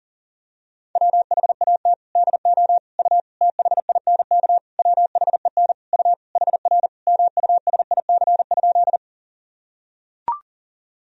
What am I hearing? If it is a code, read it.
Morse code, heard as a keyed tone, about 40 wpm, 700 Hz. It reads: WHAT DO U THINK WHEN U HR MUSIC?
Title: Sentences from Top 300 Words and Abbreviations 40wpm